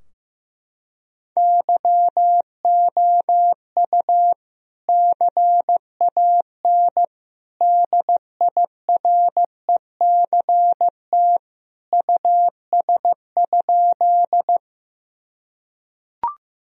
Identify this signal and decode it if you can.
Morse code, heard as a keyed tone, about 16 wpm, 700 Hz. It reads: YOU CAN DIRECT US?
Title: Sentences from Top 1000 Words - Review Part 2 - 15wpm